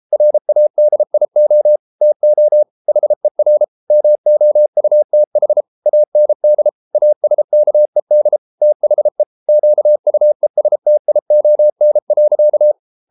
Morse code, heard as a keyed tone, about 33 wpm, 600 Hz. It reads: RADIO TO HER MOUTH AND ASKED THE QUESTION.